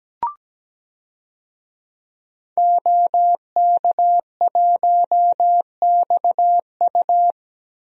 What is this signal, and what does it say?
Morse code, heard as a keyed tone, about 17 wpm, 700 Hz. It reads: OK1XU
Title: DXCC-100 Call Signs - 5 Characters - Repeated 2x - 17wpm